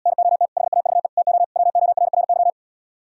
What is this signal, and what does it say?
Morse code, heard as a keyed tone, about 78 wpm, 700 Hz. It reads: NOT HAVE TO WORRY